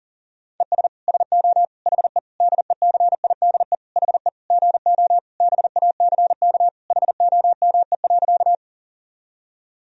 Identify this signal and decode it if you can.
Morse code, heard as a keyed tone, about 40 wpm, 700 Hz. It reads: ES SO HE DECIDE HE GO BACK HOME.
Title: Sentences from Top 500 Words and Abbreviations 40wpm